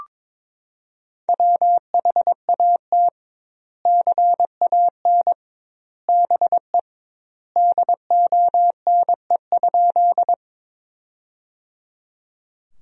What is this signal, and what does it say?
Morse code, heard as a keyed tone, about 22 wpm, 700 Hz. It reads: WHAT CAN BE DONE?
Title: 2x Word-Spacing - Sentences from Top 500 Words - 22wpm